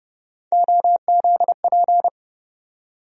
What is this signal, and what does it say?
Morse code, heard as a keyed tone, about 32 wpm, 700 Hz. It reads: OZP
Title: USA Call Sign Postfixes - 3 Letters 30wpm